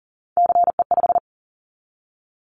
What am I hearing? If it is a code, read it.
Morse code, heard as a keyed tone, about 41 wpm, 700 Hz. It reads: CE5
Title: Callsign Prefixes for Top 100 DXCC - 40wpm